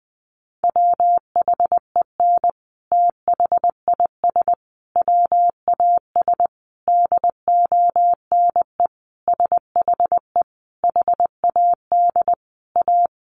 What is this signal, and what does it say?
Morse code, heard as a keyed tone, about 20 wpm, 700 Hz. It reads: WHEN THIS WAS DONE SHE HAD A
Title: Sentences from Top 900 Words 20wpm